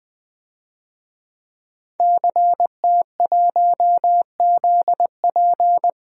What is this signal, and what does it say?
Morse code, heard as a keyed tone, about 21 wpm, 700 Hz. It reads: CT1ZP
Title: DXCC-100 Call Signs - 5 Characters - 20wpm